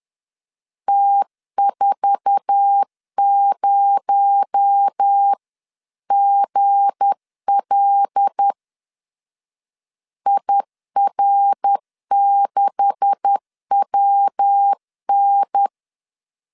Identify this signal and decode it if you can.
Morse code, heard as a keyed tone, about 10 wpm, 800 Hz. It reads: T40GL IR6WN
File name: LEZIONI / 10 SIGNS / 10B INT 10wpm 2844